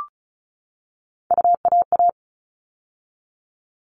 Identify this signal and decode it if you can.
Morse code, heard as a keyed tone, about 34 wpm, 700 Hz. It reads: UAA
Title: All Cut Numbers - 3-Digit 35wpm